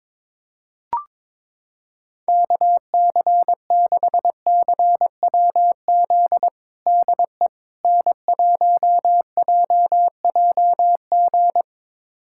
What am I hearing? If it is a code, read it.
Morse code, heard as a keyed tone, about 22 wpm, 700 Hz. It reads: KC6CWZ DE N1JJG
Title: USA Callsign DE Callsign 22wpm